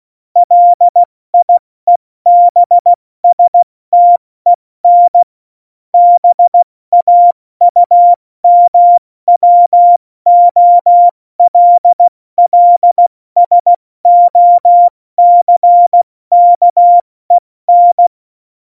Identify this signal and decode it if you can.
Morse code, heard as a keyed tone, about 16 wpm, 700 Hz. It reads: LIEBSTEN BAUMWOLLSOCKEN
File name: Morsekurs / Witze / Witz 13. Februar 2018, 70 F80